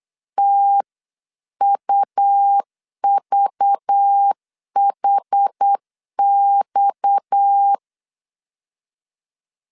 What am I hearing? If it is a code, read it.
Morse code, heard as a keyed tone, about 8 wpm, 800 Hz. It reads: TUVHX
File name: 8wpm